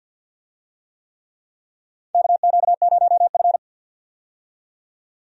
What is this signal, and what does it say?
Morse code, heard as a keyed tone, about 47 wpm, 700 Hz. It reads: KQ0F
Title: USA Call Signs - 4 Characters - 50wpm